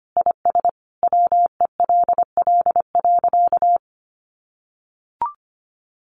Morse code, heard as a keyed tone, about 25 wpm, 700 Hz. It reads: IS WELL.
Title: Sentences from Top 800 Words 25wpm